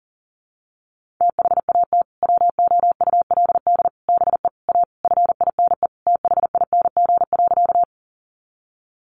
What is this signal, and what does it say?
Morse code, heard as a keyed tone, about 40 wpm, 700 Hz. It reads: THAT WOULD BE A FINE THING.
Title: Sentences from Top 500 Words 40wpm